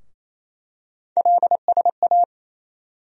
Morse code, heard as a keyed tone, about 28 wpm, 700 Hz. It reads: LSA
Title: USA Call Sign Postfixes - 3 Letters 28wpm